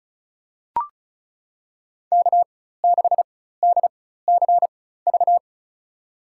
Morse code, heard as a keyed tone, about 36 wpm, spaced as for 16 wpm, 700 Hz. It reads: K6DCV
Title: USA Call Signs - 4x Character Spacing - 35wpm